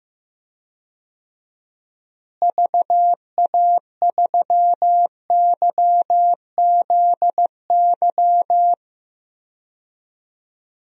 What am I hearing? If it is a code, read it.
Morse code, heard as a keyed tone, about 15 wpm, 700 Hz. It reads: VA3YZY
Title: DXCC-100 Call Signs - 6 Characters - Repeated 2x - 15wpm